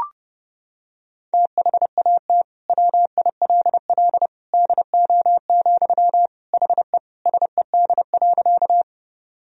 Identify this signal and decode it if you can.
Morse code, heard as a keyed tone, about 30 wpm, 700 Hz. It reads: THAT WILL DO, HE SED.
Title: Sentences from Top 100 Words and Abbreviations 30wpm